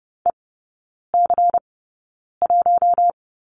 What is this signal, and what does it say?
Morse code, heard as a keyed tone, about 30 wpm, spaced as for 9 wpm, 700 Hz. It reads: EC1
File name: Lesson 16e - Letter c - Three Characters 30wpm